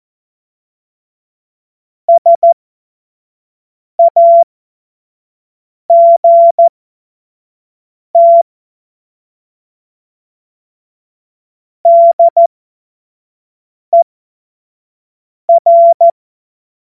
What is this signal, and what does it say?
Morse code, heard as a keyed tone, about 14 wpm, spaced as for 5 wpm, 680 Hz. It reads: SAGT DER
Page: cw-witze